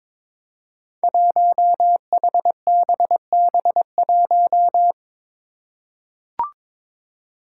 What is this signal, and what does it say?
Morse code, heard as a keyed tone, about 22 wpm, 700 Hz. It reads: 1HBB1
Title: Five Challenging Letter-Number Combinations 22wpm